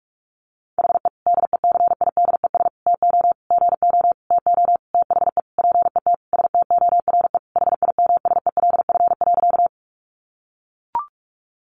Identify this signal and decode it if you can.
Morse code, heard as a keyed tone, about 44 wpm, 700 Hz. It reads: HE DECIDES TO GO TO THE PET STORE HIMSELF.
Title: Easy Phrases 45wpm